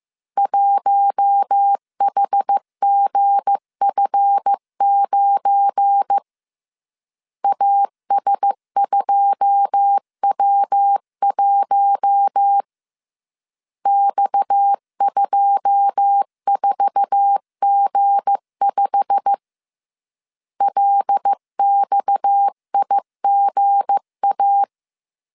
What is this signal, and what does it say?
Morse code, heard as a keyed tone, about 15 wpm, 800 Hz. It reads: 1HGF9 AS2W1 X24G5 LXIGA
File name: CW / DET / 1 9 1500C S A 14wpm 3222